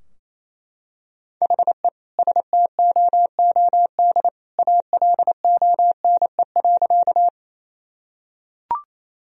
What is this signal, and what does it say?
Morse code, heard as a keyed tone, about 28 wpm, 700 Hz. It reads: HE STOOD ALONE.
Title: Sentences from Top 1000 Words - Review Part 2 - 28wpm